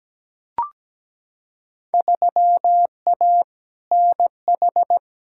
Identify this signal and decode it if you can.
Morse code, heard as a keyed tone, about 17 wpm, 700 Hz. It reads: 3A NH
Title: Field Day Exchanges Encore - 17wpm